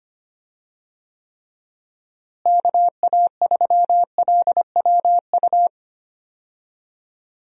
Morse code, heard as a keyed tone, about 25 wpm, 700 Hz. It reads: KA3LWU
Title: USA Call Signs - 6 Characters - Repeated 2x - 25wpm